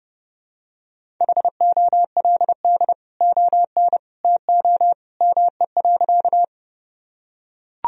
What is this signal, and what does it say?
Morse code, heard as a keyed tone, about 30 wpm, 700 Hz. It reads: HOLD ON TO ME.
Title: Sentences from Top 1000 Words - Review Part 2 - 30wpm